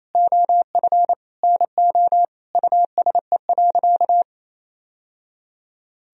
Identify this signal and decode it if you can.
Morse code, heard as a keyed tone, about 28 wpm, 700 Hz. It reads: OF NO USE.
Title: Sentences from Top 1000 Words - Review Part 4 - 28wpm